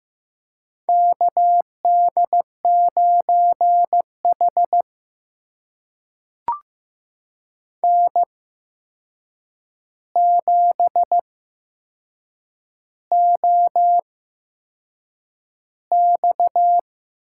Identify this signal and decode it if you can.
Morse code, heard as a keyed tone, about 15 wpm, spaced as for 6 wpm, 700 Hz. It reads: KD9H N7OX
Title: USA Call Signs - 4 Characters - 8x Character Spacing - 15wpm